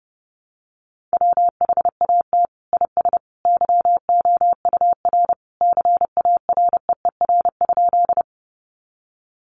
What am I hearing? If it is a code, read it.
Morse code, heard as a keyed tone, about 30 wpm, 700 Hz. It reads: WHAT IS YOUR CAREER?
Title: Sentences from Top 1000 Words 30wpm